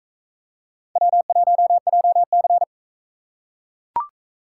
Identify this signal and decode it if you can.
Morse code, heard as a keyed tone, about 38 wpm, 700 Hz. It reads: W1JC